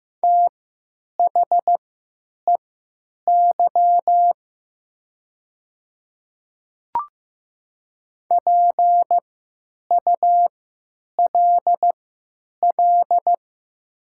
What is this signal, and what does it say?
Morse code, heard as a keyed tone, about 15 wpm, spaced as for 9 wpm, 700 Hz. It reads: THEY PULL
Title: 3x Character Spacing - Top 500 Words - 15wpm